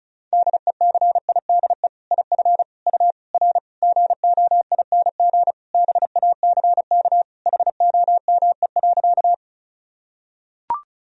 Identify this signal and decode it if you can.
Morse code, heard as a keyed tone, about 34 wpm, 700 Hz. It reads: DECIDE IF U R GOING BACK HOME.
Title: Sentences from Top 500 Words and Abbreviations 35wpm